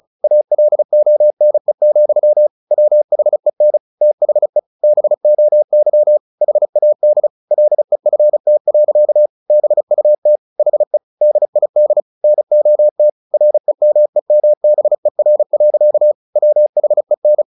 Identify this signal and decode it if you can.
Morse code, heard as a keyed tone, about 34 wpm, 600 Hz. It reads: ALONE, WHEN THE BOY HAD LEFT. BUT HE DID NOT REMEMBER. WHEN
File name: the old man and the sea / the old man and the sea 1 35wpm